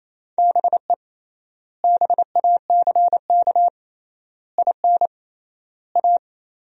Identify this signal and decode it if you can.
Morse code, heard as a keyed tone, about 28 wpm, 700 Hz. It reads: BE BACK IN A
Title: 3x Word-Spacing - Sentences from Top 500 Words - 28wpm